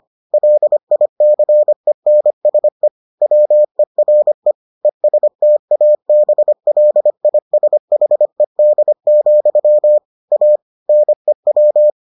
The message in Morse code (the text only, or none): LICENSE WERE ESTABLISHED, A NEW